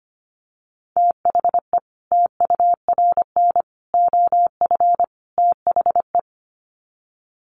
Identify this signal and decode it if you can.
Morse code, heard as a keyed tone, about 25 wpm, 700 Hz. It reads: THE TURN OF THE
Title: Sets of 4 Words 25wpm